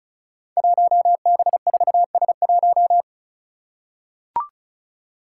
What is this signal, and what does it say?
Morse code, heard as a keyed tone, about 34 wpm, 700 Hz. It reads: 1B4S1
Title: Five Challenging Letter-Number Combinations 35wpm